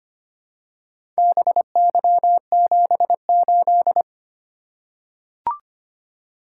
Morse code, heard as a keyed tone, about 25 wpm, 700 Hz. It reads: BY78
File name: practice / 25wpm / Four Letter-Number Combinations - 25wpm